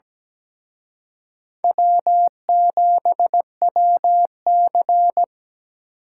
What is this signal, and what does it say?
Morse code, heard as a keyed tone, about 20 wpm, 700 Hz. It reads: W7WC
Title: Speed Racing - USA 1x2 Call Signs 17wpm